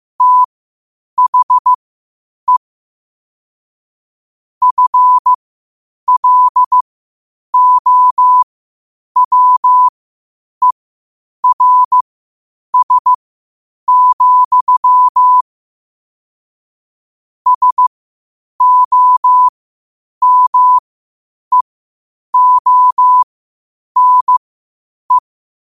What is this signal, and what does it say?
Morse code, heard as a keyed tone, about 15 wpm, spaced as for 8 wpm, 1000 Hz. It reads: THE FLOWERS, SOMEONE
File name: qotdfiles / 8wpm